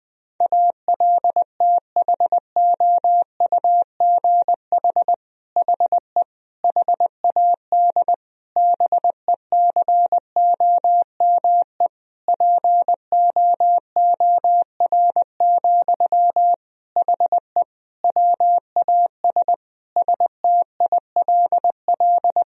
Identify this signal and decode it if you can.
Morse code, heard as a keyed tone, about 20 wpm, 700 Hz. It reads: ALTHOUGH HE HAD BECOME POOR, HE WAS STILL
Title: Sentences from Top 1000 Words - Review Part 3 - 20wpm